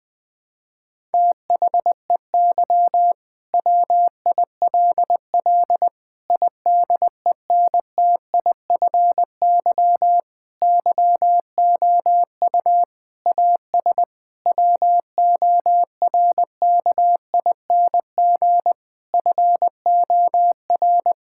Morse code, heard as a keyed tone, about 19 wpm, 700 Hz. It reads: THEY WILL IDENTIFY YOU AS WORKING FOR
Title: Sentences from Top 1000 Words - Review Part 6 - 20wpm